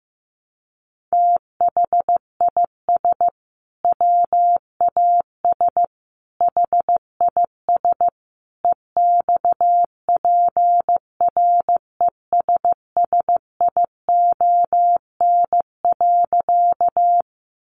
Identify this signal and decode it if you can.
Morse code, heard as a keyed tone, about 15 wpm, 700 Hz. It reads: THIS WAS HIS EXPRESSION.
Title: Sentences from Top 1200 Words 15wpm